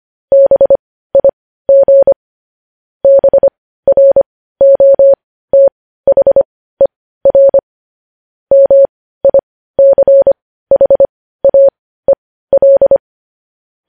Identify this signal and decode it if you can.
Morse code, heard as a keyed tone, about 25 wpm, spaced as for 15 wpm, 550 Hz. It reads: BIG BROTHER MICHAEL
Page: news-headlines-in-morse-code-at-15-wpm